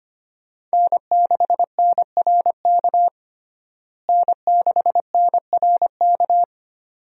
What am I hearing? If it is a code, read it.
Morse code, heard as a keyed tone, about 25 wpm, 700 Hz. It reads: N6NRK N6NRK
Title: USA Call Signs - 5 Characters - Repeated 2x - 25wpm